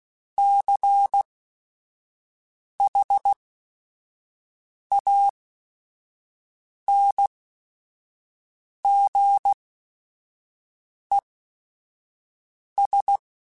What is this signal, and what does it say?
Morse code, heard as a keyed tone, about 16 wpm, spaced as for 5 wpm, 800 Hz. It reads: CHANGES
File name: Technical Subjects / Technical Subjects-mp3s / Reactance @5